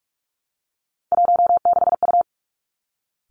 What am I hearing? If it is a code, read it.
Morse code, heard as a keyed tone, about 46 wpm, 700 Hz. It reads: 16U
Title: Three Challenging Letter-Number Combinations 45wpm